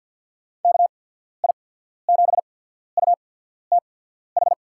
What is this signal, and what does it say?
Morse code, heard as a keyed tone, about 53 wpm, spaced as for 14 wpm, 690 Hz. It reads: KI7UTH
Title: USA Call Signs - 6 Characters - 8x Character Spacing - 50wpm